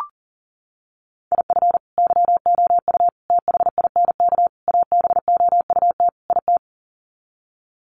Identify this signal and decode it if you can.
Morse code, heard as a keyed tone, about 40 wpm, 700 Hz. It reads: IF YOU THINK ABOUT IT
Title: Sets of 5 Words Encore2 40wpm